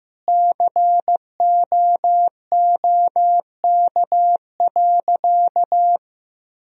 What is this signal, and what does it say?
Morse code, heard as a keyed tone, about 15 wpm, 700 Hz. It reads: COOK.
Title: Sentences from Top 1000 Words - Review Part 4 - 15wpm